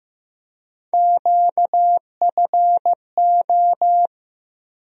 Morse code, heard as a keyed tone, about 15 wpm, 700 Hz. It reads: QFO
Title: USA Call Sign Postfixes - 3 Letters 15wpm